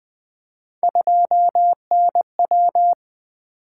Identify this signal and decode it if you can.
Morse code, heard as a keyed tone, about 20 wpm, 700 Hz. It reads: 2NW
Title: Three Letter-Number Combinations - 20wpm